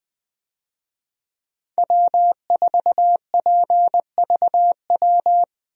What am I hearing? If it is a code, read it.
Morse code, heard as a keyed tone, about 20 wpm, 700 Hz. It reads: W4PVW
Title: USA Call Signs - 5 Characters - Repeated 3x - 20wpm